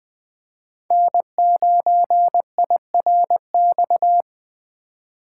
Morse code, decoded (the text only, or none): N9IRX